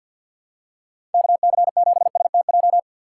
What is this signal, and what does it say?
Morse code, heard as a keyed tone, about 46 wpm, 700 Hz. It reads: KC7STJ
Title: USA Call Signs - 6 Characters - 3x Character Spacing - 50wpm